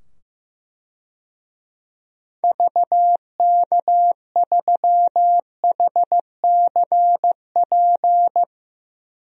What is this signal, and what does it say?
Morse code, heard as a keyed tone, about 15 wpm, 700 Hz. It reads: VK3HCP
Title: DXCC-100 Call Signs - 6 Characters - 15wpm